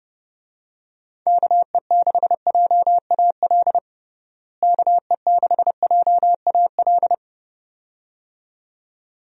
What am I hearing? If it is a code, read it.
Morse code, heard as a keyed tone, about 30 wpm, 700 Hz. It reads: KE6JAL KE6JAL
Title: USA Call Signs - 6 Characters - Repeated 2x - 30wpm